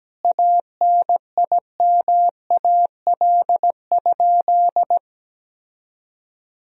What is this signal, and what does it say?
Morse code, heard as a keyed tone, about 17 wpm, 700 Hz. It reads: ANIMAL?
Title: Sentences from Top 1000 Words - Review Part 1 - 17wpm